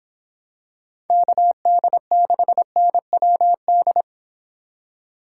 Morse code, heard as a keyed tone, about 25 wpm, 700 Hz. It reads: KD6NWD
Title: Speed Racing - USA 2x3 Call Signs 22wpm